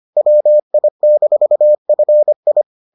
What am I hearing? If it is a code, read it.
Morse code, heard as a keyed tone, about 25 wpm, 600 Hz. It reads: WI-FI